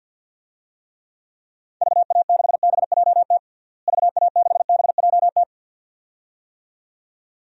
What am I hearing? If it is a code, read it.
Morse code, heard as a keyed tone, about 53 wpm, 700 Hz. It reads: VA6BJT VA6BJT
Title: DXCC-100 Call Signs - 6 Characters - Repeated 2x - 50wpm